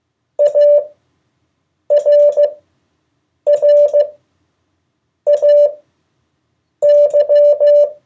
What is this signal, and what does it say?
Morse code, heard as a keyed tone, about 15 wpm, spaced as for 6 wpm, 580 Hz. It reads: ARRAY
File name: 2of4-words-1